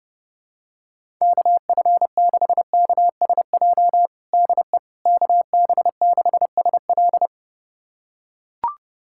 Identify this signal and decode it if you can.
Morse code, heard as a keyed tone, about 30 wpm, 700 Hz. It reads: KF6KSJ DE KB6SL
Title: USA Callsign DE Callsign 30wpm